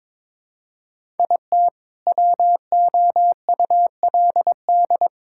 Morse code, heard as a keyed tone, about 22 wpm, 700 Hz. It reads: IT WOULD
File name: practice / 22wpm / Sentences with 2-Word Phrases - 22wpm